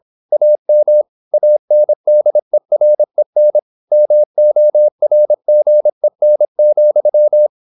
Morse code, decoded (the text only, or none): AM ANDEREN MORGEN,